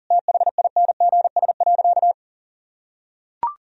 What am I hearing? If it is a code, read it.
Morse code, heard as a keyed tone, about 40 wpm, 700 Hz. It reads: THINGS.
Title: Sentences from Top 1000 Words - Review Part 4 - 40wpm